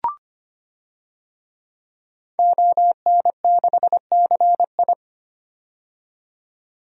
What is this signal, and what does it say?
Morse code, heard as a keyed tone, about 25 wpm, 700 Hz. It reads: ON6CI
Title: DXCC-100 Call Signs - 5 Characters - 25wpm